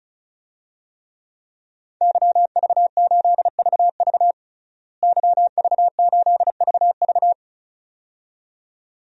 Morse code, heard as a keyed tone, about 34 wpm, 700 Hz. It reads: YV8VV YV8VV
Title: DXCC-100 Call Signs - 5 Characters - Repeated 2x - 35wpm